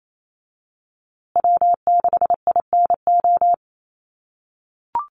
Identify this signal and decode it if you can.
Morse code, heard as a keyed tone, about 28 wpm, 700 Hz. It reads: W6INO
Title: USA Call Signs Repeated 2x 28wpm